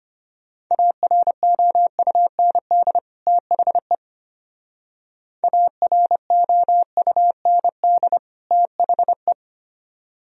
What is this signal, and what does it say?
Morse code, heard as a keyed tone, about 31 wpm, 700 Hz. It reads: AROUND THE AROUND THE
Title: Speed-Racing - Sets of 2 Words 25wpm